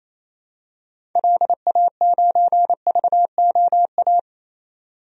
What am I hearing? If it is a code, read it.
Morse code, heard as a keyed tone, about 28 wpm, 700 Hz. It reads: LA9VOA
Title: DXCC-100 Call Signs - 6 Characters - 28wpm